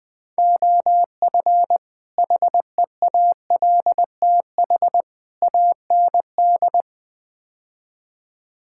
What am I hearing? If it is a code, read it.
Morse code, heard as a keyed tone, about 20 wpm, 700 Hz. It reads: OF HEALTH AND
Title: Build A Phrase - 5-Word Phrases - 20wpm